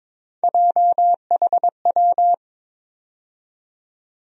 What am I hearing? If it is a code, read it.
Morse code, heard as a keyed tone, about 22 wpm, 700 Hz. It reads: JHW
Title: USA Call Sign Postfixes - 3 Letters 22wpm